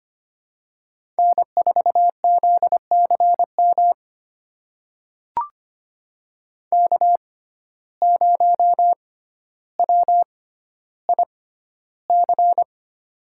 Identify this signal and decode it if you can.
Morse code, heard as a keyed tone, about 25 wpm, spaced as for 13 wpm, 700 Hz. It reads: N4ZCM K0WIC
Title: USA Call Signs - 5 Characters - 6x Character Spacing - 25wpm